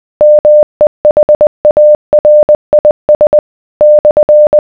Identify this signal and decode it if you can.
Morse code, heard as a keyed tone, about 20 wpm, 600 Hz. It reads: MEHARIS /